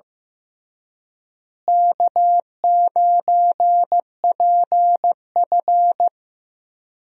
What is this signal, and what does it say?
Morse code, heard as a keyed tone, about 15 wpm, 700 Hz. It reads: K9PF
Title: USA Call Signs - 4 Characters - Repeated 3x - 15wpm